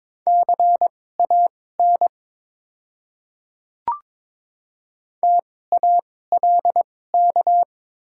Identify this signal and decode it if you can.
Morse code, heard as a keyed tone, about 22 wpm, spaced as for 16 wpm, 700 Hz. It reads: CAN TALK
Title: 2x Character Spacing - Top 500 Words - 22wpm